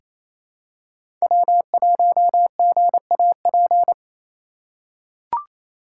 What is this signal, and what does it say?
Morse code, heard as a keyed tone, about 28 wpm, 700 Hz. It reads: W1GAP
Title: USA Call Signs - 5 Characters - 2x Character Spacing - 28wpm